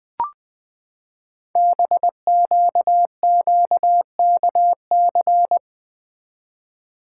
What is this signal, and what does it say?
Morse code, heard as a keyed tone, about 20 wpm, 700 Hz. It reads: BQQKC
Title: Code Groups - 5 Letters - 20wpm